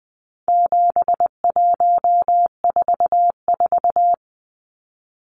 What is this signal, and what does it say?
Morse code, heard as a keyed tone, about 20 wpm, 700 Hz. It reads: 7144
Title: Numbers - 4-Digit 20wpm